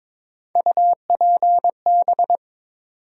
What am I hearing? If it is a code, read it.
Morse code, heard as a keyed tone, about 22 wpm, 700 Hz. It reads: UPB
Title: USA Call Sign Postfixes - 3 Letters 22wpm